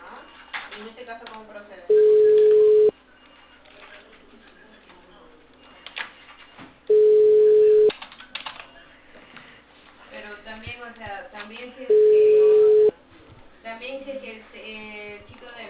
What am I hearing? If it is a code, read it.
Ringback tone (Europe, 425 Hz), with speech.